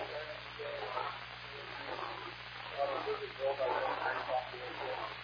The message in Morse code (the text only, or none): none